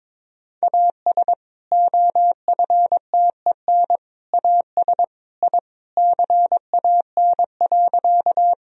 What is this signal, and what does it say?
Morse code, heard as a keyed tone, about 22 wpm, 700 Hz. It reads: AS OFTEN AS I CAN.
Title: Sentences from Top 1000 Words - Review Part 1 - 22wpm